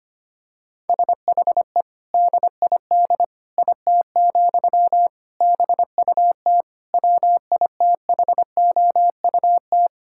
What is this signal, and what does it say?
Morse code, heard as a keyed tone, about 25 wpm, 700 Hz. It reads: SHE DID IT, BUT WITHOUT
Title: Sentences from Top 1000 Words - Review Part 6 - 25wpm